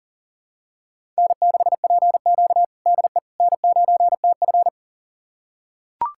Morse code, heard as a keyed tone, about 40 wpm, 700 Hz. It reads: N6PQ DE N9TF
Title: USA Callsign DE Callsign - 4 Characters - 40wpm